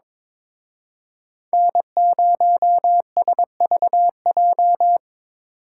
Morse code, heard as a keyed tone, about 28 wpm, 700 Hz. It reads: N0SVJ